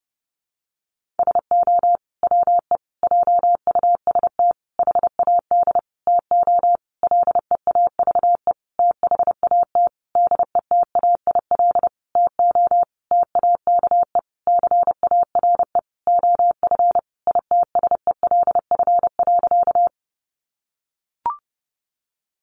Morse code, heard as a keyed tone, about 30 wpm, 700 Hz. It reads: SO WE JUST HAD TO LEAVE THAT DETAIL TO TAKE CARE OF ITSELF.
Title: Sentences from Top 1500 Words 30wpm